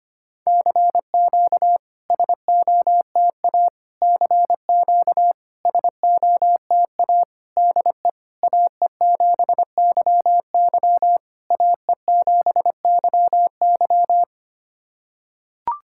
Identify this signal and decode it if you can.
Morse code, heard as a keyed tone, about 25 wpm, 700 Hz. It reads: CQ SOTA CQ SOTA DE AE7YY AE7YY
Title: SOTA QSO Practice 25wpm